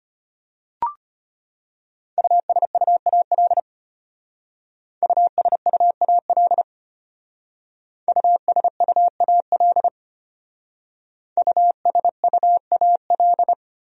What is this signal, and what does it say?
Morse code, heard as a keyed tone, about 32 wpm, 700 Hz. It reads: USUAL USUAL USUAL USUAL